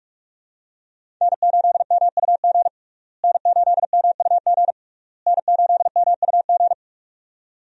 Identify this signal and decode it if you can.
Morse code, heard as a keyed tone, about 44 wpm, 700 Hz. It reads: N8MUG N8MUG N8MUG
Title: USA Call Signs - 5 Characters - Repeated 3x - 45wpm